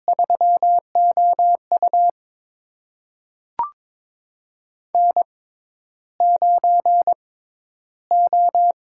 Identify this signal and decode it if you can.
Morse code, heard as a keyed tone, about 22 wpm, spaced as for 11 wpm, 700 Hz. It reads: 3OU N9O
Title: USA Call Signs - 6x Character Spacing - 22wpm